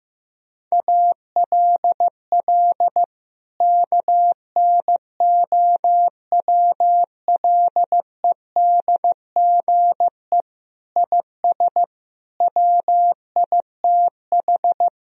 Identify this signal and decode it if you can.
Morse code, heard as a keyed tone, about 16 wpm, 700 Hz. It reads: ALL KNOWLEDGE IS WITH
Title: Sentences from Top 1000 Words - Review Part 3 - 15wpm